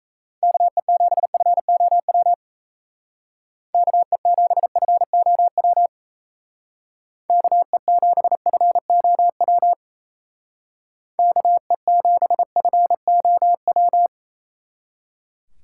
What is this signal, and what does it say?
Morse code, heard as a keyed tone, about 36 wpm, 700 Hz. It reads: KE7FOW KE7FOW KE7FOW KE7FOW